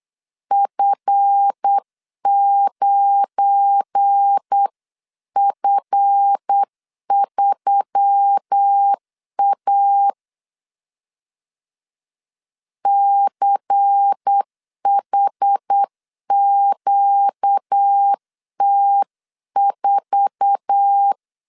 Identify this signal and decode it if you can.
Morse code, heard as a keyed tone, about 8 wpm, 800 Hz. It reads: F9F3A CHQT4
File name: CW / DET / 1 9 1500C S A 8wpm 5637